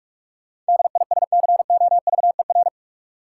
Morse code, heard as a keyed tone, about 46 wpm, 700 Hz. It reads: DISCOVER
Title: Sentences from Top 1100 Words - New Word Drill Alt - 45wpm